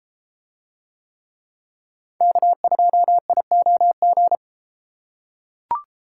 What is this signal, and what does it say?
Morse code, heard as a keyed tone, about 31 wpm, 700 Hz. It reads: K2IOG